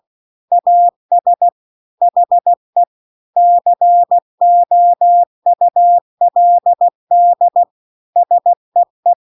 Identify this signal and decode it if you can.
Morse code, heard as a keyed tone, about 16 wpm, 700 Hz. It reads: AS HE COULD SEE